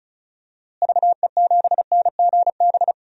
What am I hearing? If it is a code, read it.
Morse code, heard as a keyed tone, about 36 wpm, 700 Hz. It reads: VE7NGB
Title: DXCC-100 Call Signs - 6 Characters - Repeated 2x - 35wpm